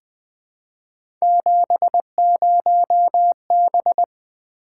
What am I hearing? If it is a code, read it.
Morse code, heard as a keyed tone, about 20 wpm, 700 Hz. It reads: 70B
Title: Three Letter-Number Combinations - 20wpm